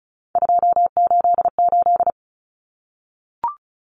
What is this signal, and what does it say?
Morse code, heard as a keyed tone, about 34 wpm, 700 Hz. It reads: 288
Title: Common Cut Numbers - 3-Digit 35wpm